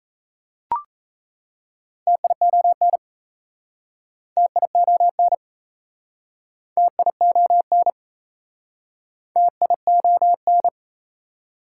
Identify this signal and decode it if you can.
Morse code, heard as a keyed tone, about 38 wpm, 690 Hz. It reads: TION TION TION TION